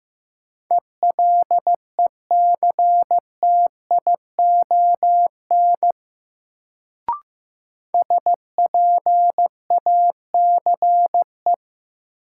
Morse code, heard as a keyed tone, about 15 wpm, 700 Hz. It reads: ELECTION SPACE
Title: Sentences from Top 1500 Words - New Word Drill Alt - 15wpm